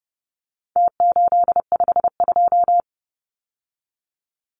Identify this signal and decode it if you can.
Morse code, heard as a keyed tone, about 30 wpm, 700 Hz. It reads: T852
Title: Numbers - Contest Style - 4-Digit 30wpm